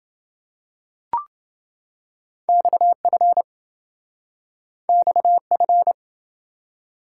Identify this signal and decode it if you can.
Morse code, heard as a keyed tone, about 27 wpm, 700 Hz. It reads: XF XF